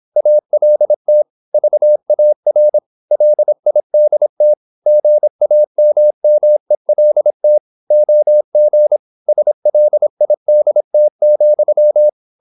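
Morse code, heard as a keyed tone, about 26 wpm, 600 Hz. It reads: ALT VAR LIDT GAMMELT OG SLIDT,
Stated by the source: High-speed (QRQ) Morse at 26 wpm.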